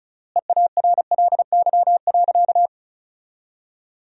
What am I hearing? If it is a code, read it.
Morse code, heard as a keyed tone, about 36 wpm, 700 Hz. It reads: EARLY.
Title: 3x Word-Spacing - Sentences from Top 400 Words - 35wpm